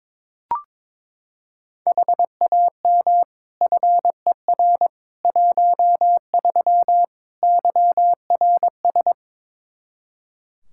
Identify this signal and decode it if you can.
Morse code, heard as a keyed tone, about 22 wpm, 700 Hz. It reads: HAM FER 13 YRS
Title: Practice QSO - Exchange 3 - More Info - 22wpm